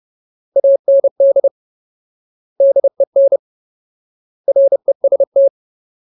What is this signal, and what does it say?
Morse code, heard as a keyed tone, about 30 wpm, 550 Hz. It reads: AND DEN REST
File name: Ultimate Word List / 30WPM Ultimate